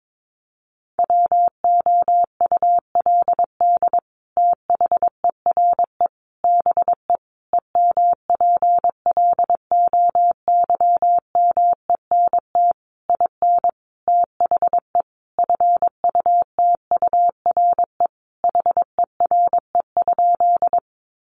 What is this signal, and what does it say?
Morse code, heard as a keyed tone, about 22 wpm, 700 Hz. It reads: WOULD THERE BE EMPLOYMENT IN THE FUTURE HERE?
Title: Sentences from Top 1500 Words 22wpm